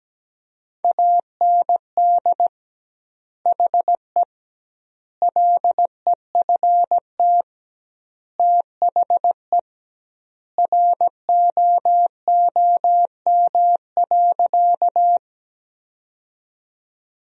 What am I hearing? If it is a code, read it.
Morse code, heard as a keyed tone, about 17 wpm, 700 Hz. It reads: AND HE LEFT THE ROOM.
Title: 2x Word-Spacing - Sentences from Top 300 Words - 17wpm